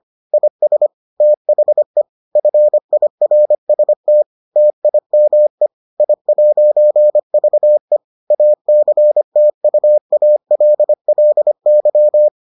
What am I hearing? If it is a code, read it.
Morse code, heard as a keyed tone, about 25 wpm, 600 Hz. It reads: IS THE FIRST TIME I'VE ACTUALLY